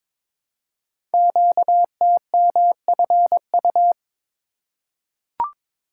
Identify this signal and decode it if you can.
Morse code, heard as a keyed tone, about 22 wpm, 700 Hz. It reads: QTMFU
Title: Code Groups - 5 Letters - 22wpm